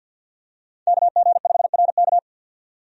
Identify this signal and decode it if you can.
Morse code, heard as a keyed tone, about 52 wpm, 700 Hz. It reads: KG5RK
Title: USA Call Signs - 5 Characters - 50wpm